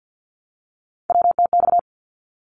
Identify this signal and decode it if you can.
Morse code, heard as a keyed tone, about 50 wpm, 700 Hz. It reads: WTX
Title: ARRL Sections 50wpm